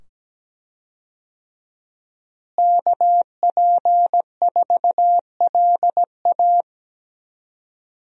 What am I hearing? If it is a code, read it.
Morse code, heard as a keyed tone, about 17 wpm, 700 Hz. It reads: KP4LA
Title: USA Call Signs - 5 Characters - Repeated 2x - 17wpm